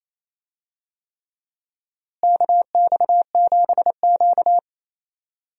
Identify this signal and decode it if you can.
Morse code, heard as a keyed tone, about 28 wpm, 700 Hz. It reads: KX7Q
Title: USA Call Signs - 4 Characters - Repeated 3x - 28wpm